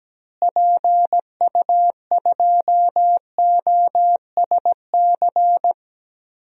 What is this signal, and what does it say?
Morse code, heard as a keyed tone, about 17 wpm, 700 Hz. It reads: PU2OSC
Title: DXCC-100 Call Signs - 6 Characters - Repeated 2x - 17wpm